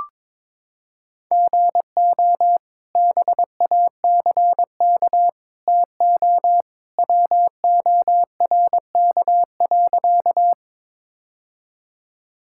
Morse code, heard as a keyed tone, about 22 wpm, 700 Hz. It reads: GO BACK TO WORK.
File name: practice / 22wpm / Sentences with 2-Word Phrases - 22wpm